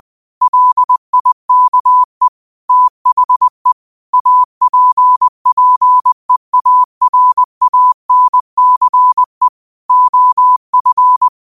Morse code, heard as a keyed tone, about 20 wpm, 1000 Hz. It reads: LIKE THE APPEARANCE OF